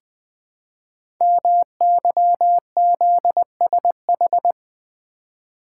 Morse code, heard as a keyed tone, about 20 wpm, 700 Hz. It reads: MYZSH
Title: Code Groups - 5 Letter:Numbers - 20wpm